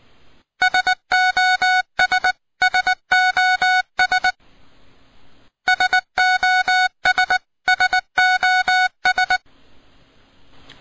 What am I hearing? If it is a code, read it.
Morse code, heard as a keyed tone, about 19 wpm, 1480 Hz. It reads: SOS SOS SOS SOS
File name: py2mok Codigos